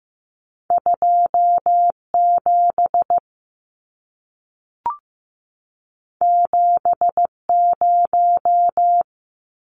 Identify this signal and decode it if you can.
Morse code, heard as a keyed tone, about 15 wpm, 700 Hz. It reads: 2 7 7 0
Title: Numbers - 2-Digit 15wpm